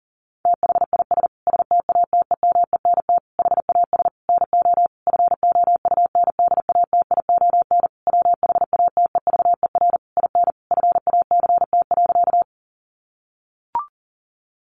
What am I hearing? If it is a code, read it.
Morse code, heard as a keyed tone, about 40 wpm, 700 Hz. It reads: THIS STATEMENT HAS NO FOUNDATION WHATEVER IN FACT.
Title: Sentences from Top 1200 Words 40wpm